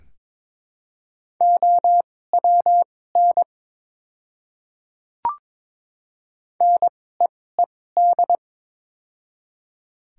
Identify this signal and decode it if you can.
Morse code, heard as a keyed tone, about 23 wpm, spaced as for 16 wpm, 700 Hz. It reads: OWN NEED
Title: 2x Character Spacing - Top 300 Words - 22wpm